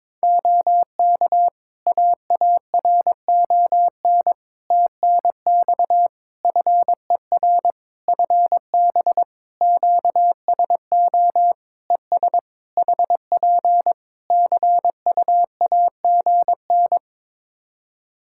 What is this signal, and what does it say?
Morse code, heard as a keyed tone, about 22 wpm, 700 Hz. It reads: OK AARON TNX FER FB QSO ES HP CUAGN
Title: K1USN SST QSO Practice 22wpm